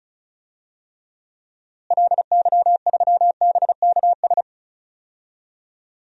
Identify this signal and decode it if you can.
Morse code, heard as a keyed tone, about 36 wpm, 700 Hz. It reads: LY3BKS
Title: DXCC-100 Call Signs - 6 Characters - Repeated 3x - 35wpm